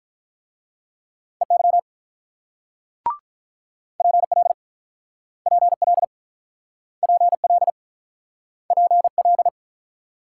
Morse code, heard as a keyed tone, about 48 wpm, 690 Hz. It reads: EY PL PL PL PL